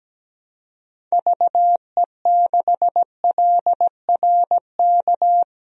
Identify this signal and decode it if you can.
Morse code, heard as a keyed tone, about 17 wpm, 700 Hz. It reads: VE6LRK
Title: DXCC-100 Call Signs - 6 Characters - 17wpm